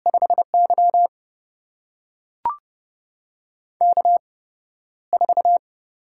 Morse code, heard as a keyed tone, about 31 wpm, spaced as for 13 wpm, 700 Hz. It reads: 5Y K4
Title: USA Call Signs - 4 Characters - 8x Character Spacing - 30wpm